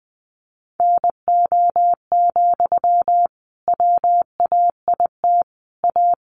Morse code, heard as a keyed tone, about 19 wpm, 700 Hz. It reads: NO, WAIT A
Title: Sentences from Top 900 Words 20wpm